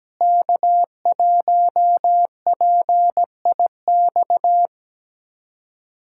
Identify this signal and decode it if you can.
Morse code, heard as a keyed tone, about 17 wpm, 700 Hz. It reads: K1PIX
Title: USA Callsign DE Callsign - 5 Characters - 17wpm